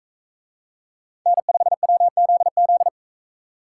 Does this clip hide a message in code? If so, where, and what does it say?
Yes: Morse code from 1.3 to 2.9 s, heard as a keyed tone, about 42 wpm, 700 Hz. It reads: N5WZZ